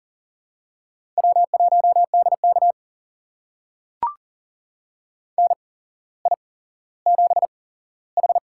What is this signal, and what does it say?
Morse code, heard as a keyed tone, about 38 wpm, spaced as for 17 wpm, 700 Hz. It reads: W1DK NI7H
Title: USA Call Signs - 8x Character Spacing - 40wpm